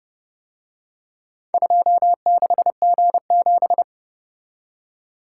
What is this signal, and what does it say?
Morse code, heard as a keyed tone, about 29 wpm, 700 Hz. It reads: 26G7